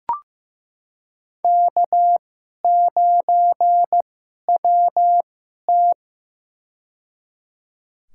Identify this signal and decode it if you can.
Morse code, heard as a keyed tone, about 15 wpm, spaced as for 11 wpm, 700 Hz. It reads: K9WT
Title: USA Call Signs - 4 Characters - 2x Character Spacing - 15wpm